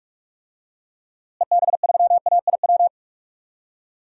Morse code, heard as a keyed tone, about 43 wpm, 700 Hz. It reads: EB3AIW
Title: DXCC-100 Call Signs - 6 Characters - Repeated 3x - 45wpm